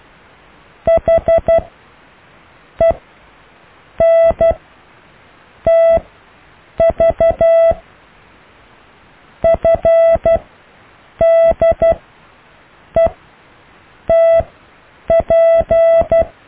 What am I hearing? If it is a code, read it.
Morse code, heard as a keyed tone, about 12 wpm, spaced as for 6 wpm, 660 Hz. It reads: HENTV FDETP
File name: cw / EKTPV+BFHNW